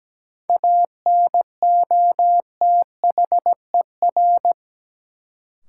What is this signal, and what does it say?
Morse code, heard as a keyed tone, about 17 wpm, 700 Hz. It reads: ANOTHER
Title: Sets of 5 Words Encore - New Word Drill - 17wpm